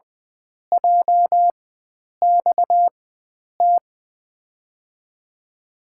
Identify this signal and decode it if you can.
Morse code, heard as a keyed tone, about 20 wpm, spaced as for 9 wpm, 700 Hz. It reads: JXT
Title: USA Call Signs - 6 Characters - 4x Character Spacing - 20wpm